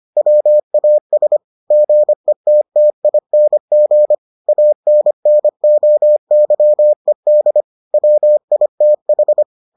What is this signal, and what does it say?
Morse code, heard as a keyed tone, about 25 wpm, 600 Hz. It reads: WAS GETTING ANNOYED WITH